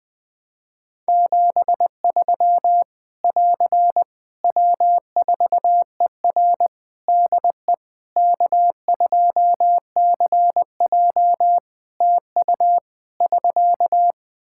Morse code, heard as a keyed tone, about 20 wpm, 700 Hz. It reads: 73 <AR> W4ER DE K2CJ TU <SK>
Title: Practice QSO - Exchange 4 - Ending - 20wpm